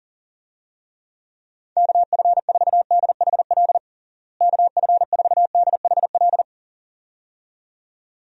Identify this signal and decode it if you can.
Morse code, heard as a keyed tone, about 37 wpm, 700 Hz. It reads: KF4DHL KF4DHL
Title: USA Call Signs - 6 Characters - Repeated 2x - 40wpm